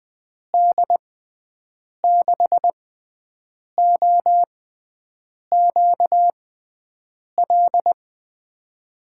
Morse code, heard as a keyed tone, about 20 wpm, spaced as for 7 wpm, 700 Hz. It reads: D6OQL
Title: USA Call Signs - 6 Characters - 6x Character Spacing - 20wpm